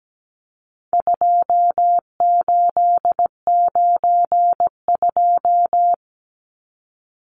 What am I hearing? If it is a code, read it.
Morse code, heard as a keyed tone, about 17 wpm, 700 Hz. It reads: 2892
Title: Numbers - 4-Digit 17wpm